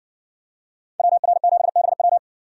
Morse code, heard as a keyed tone, about 58 wpm, 700 Hz. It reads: WR7BW